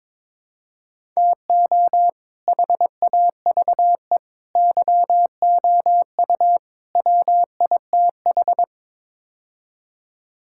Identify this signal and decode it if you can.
Morse code, heard as a keyed tone, about 22 wpm, 700 Hz. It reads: TO HAVE YOU WITH
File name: practice / 22wpm / Build A Phrase - 5-Word Phrases - 22wpm